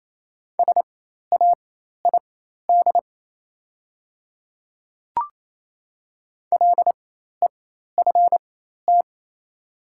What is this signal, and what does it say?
Morse code, heard as a keyed tone, about 28 wpm, spaced as for 13 wpm, 700 Hz. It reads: SAID LEFT
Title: 4x Character Spacing - Top 300 Words - 28wpm